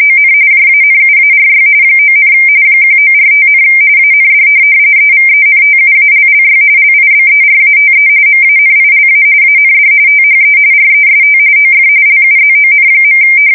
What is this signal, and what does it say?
RTTY, about 45.45 baud. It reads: ON JANUARY 9, 1945, AND EVENTUALLY LIBERATE MANILA AND THE REST OF LUZON.
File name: fsk170 / USS-New-Jersey